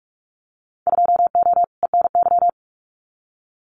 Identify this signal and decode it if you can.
Morse code, heard as a keyed tone, about 46 wpm, 700 Hz. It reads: 2O ENY
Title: Winter Field Day Exchanges 45wpm